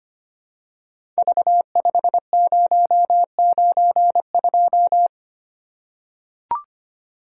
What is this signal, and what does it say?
Morse code, heard as a keyed tone, about 25 wpm, 700 Hz. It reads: V5092
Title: Code Groups - 5 Letter:Numbers - 25wpm